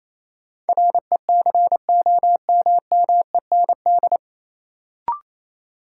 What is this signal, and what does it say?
Morse code, heard as a keyed tone, about 28 wpm, 700 Hz. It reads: RECOMMEND
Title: Sentences from Top 1200 Words - New Word Drill Alt - 28wpm